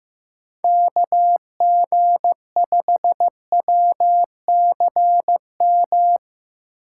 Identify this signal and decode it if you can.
Morse code, heard as a keyed tone, about 15 wpm, 700 Hz. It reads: KG5WCM
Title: USA Call Signs - 6 Characters - 2x Character Spacing - 15wpm